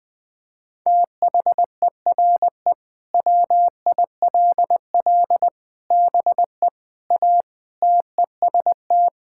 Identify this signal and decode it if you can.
Morse code, heard as a keyed tone, about 21 wpm, 700 Hz. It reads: THERE WILL BE A TEST
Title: Sentences with 3-Word Phrases - Spoken Beginning - 20wpm